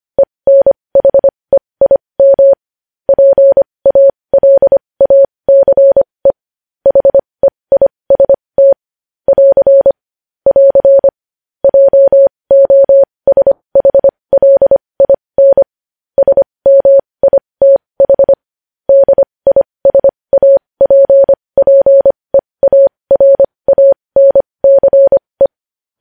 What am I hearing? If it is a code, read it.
Morse code, heard as a keyed tone, about 25 wpm, spaced as for 20 wpm, 550 Hz. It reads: ENHEIM PALACE HEIST <AR> <AR> JOSHLIN SMITH DISAPPEARANCE
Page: news-headlines-in-morse-code-at-20-wpm